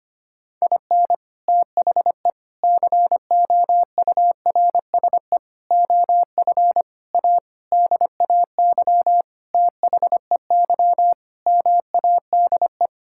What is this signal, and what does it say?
Morse code, heard as a keyed tone, about 25 wpm, 700 Hz. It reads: IN THE COURSE OF A DAY THEY MADE
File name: practice / 25wpm / Sentences with 4-Word Phrases - 25wpm